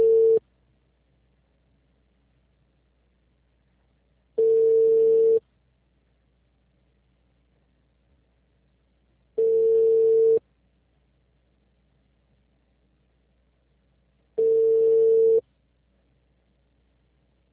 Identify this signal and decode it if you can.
Ringback tone (China, 450 Hz).
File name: out- -2002- - - .176